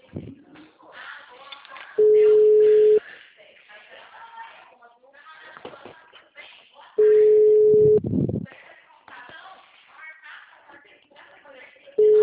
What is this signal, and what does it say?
Ringback tone (Europe, 425 Hz), with speech.
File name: out- -2010- - - .32412